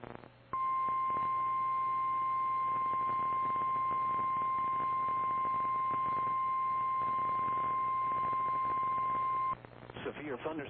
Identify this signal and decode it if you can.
Emergency Alert System (SAME) tones (the 1050 Hz attention tone), with speech. It is heard from 0.5 to 9.6 s.
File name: eas / KKPRKXPN / KKPRKXPN